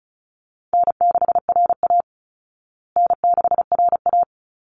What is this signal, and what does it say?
Morse code, heard as a keyed tone, about 34 wpm, 700 Hz. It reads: N6RA N6RA
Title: Easy Call Signs Repeated 3x - 35wpm